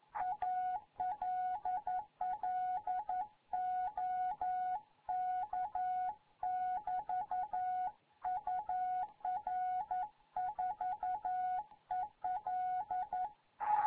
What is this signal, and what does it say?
Morse code, heard as a keyed tone, about 11 wpm, 740 Hz. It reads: ALL OK <BT> UR 4EL
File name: Morze-HA2MN